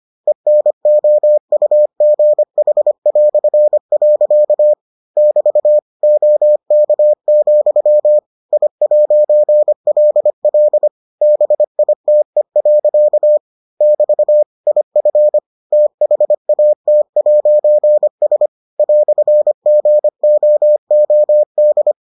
Morse code, heard as a keyed tone, about 25 wpm, 600 Hz. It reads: ENOUGH". <BT> OK, I'LL BITE. <BT> IF THAT'S "GOOD